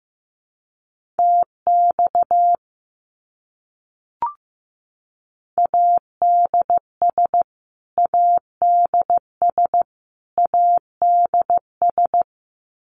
Morse code, heard as a keyed tone, about 14 wpm, 700 Hz. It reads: TX ADS ADS ADS
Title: Abbreviations Repeated 3x - 15wpm